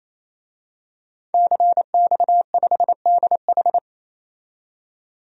Morse code, heard as a keyed tone, about 28 wpm, 700 Hz. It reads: CX5DH
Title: DXCC-100 Call Signs - 5 Characters - 28wpm